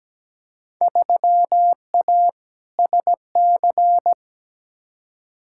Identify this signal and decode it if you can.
Morse code, heard as a keyed tone, about 17 wpm, 700 Hz. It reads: 3A SC
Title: Field Day Exchanges Encore - 17wpm